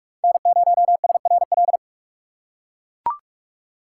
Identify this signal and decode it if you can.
Morse code, heard as a keyed tone, about 44 wpm, 700 Hz. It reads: N0SRL
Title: USA Call Signs - 5 Characters - 6x Character Spacing - 45wpm